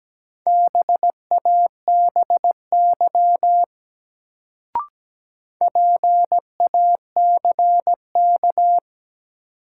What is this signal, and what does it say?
Morse code, heard as a keyed tone, about 17 wpm, 700 Hz. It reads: BABY PACK
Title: Sentences from Top 1300 Words - New Word Drill - 17wpm